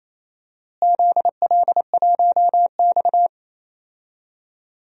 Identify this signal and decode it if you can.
Morse code, heard as a keyed tone, about 27 wpm, 700 Hz. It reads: ZL1X
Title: DXCC-100 Call Signs - 4 Characters - Repeated 3x - 28wpm